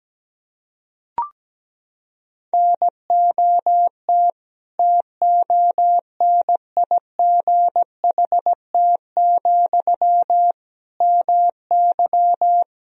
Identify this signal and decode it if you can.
Morse code, heard as a keyed tone, about 17 wpm, 700 Hz. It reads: NOT TONIGHT, MY
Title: Sentences from Top 1000 Words - Review Part 3 - 17wpm